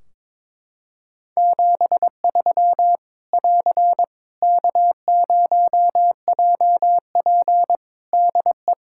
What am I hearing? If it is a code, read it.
Morse code, heard as a keyed tone, about 22 wpm, 700 Hz. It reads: 73 <AR> K0JP DE
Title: Practice QSO - Exchange 4 - Ending - 22wpm